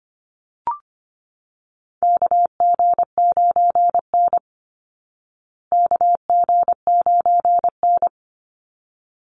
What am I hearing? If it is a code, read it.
Morse code, heard as a keyed tone, about 25 wpm, 700 Hz. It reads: KG9N KG9N
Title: Call Signs Repeated 3x - 25wpm